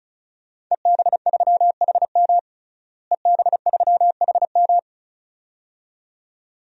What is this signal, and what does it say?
Morse code, heard as a keyed tone, about 36 wpm, 700 Hz. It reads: EB3HM EB3HM
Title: DXCC-100 Call Signs - 5 Characters - Repeated 2x - 35wpm